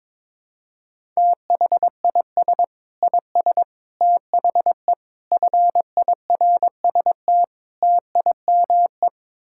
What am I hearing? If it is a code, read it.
Morse code, heard as a keyed tone, about 22 wpm, 700 Hz. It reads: THIS IS THE FIRST TIME
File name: practice / 22wpm / Build A Phrase - 5-Word Phrases - 22wpm